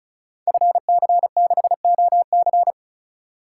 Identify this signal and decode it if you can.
Morse code, heard as a keyed tone, about 36 wpm, 700 Hz. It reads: FC6OC
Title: Code Groups - 5 Letter:Numbers - 35wpm